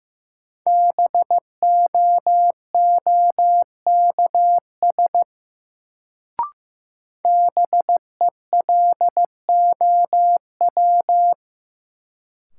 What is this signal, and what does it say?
Morse code, heard as a keyed tone, about 15 wpm, 700 Hz. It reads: BOOKS BELOW
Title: Sentences from Top 1200 Words - New Word Drill Alt - 15wpm